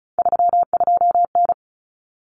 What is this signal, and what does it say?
Morse code, heard as a keyed tone, about 34 wpm, 700 Hz. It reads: 32N
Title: Common Cut Numbers - 3-Digit 35wpm